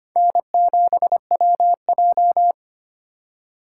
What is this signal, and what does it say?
Morse code, heard as a keyed tone, about 32 wpm, 700 Hz. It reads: N7WJ